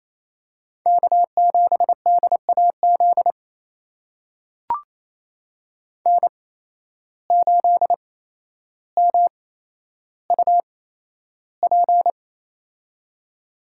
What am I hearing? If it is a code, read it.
Morse code, heard as a keyed tone, about 28 wpm, spaced as for 12 wpm, 700 Hz. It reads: K7DAZ N8MUP
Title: USA Call Signs - 5 Characters - 8x Character Spacing - 28wpm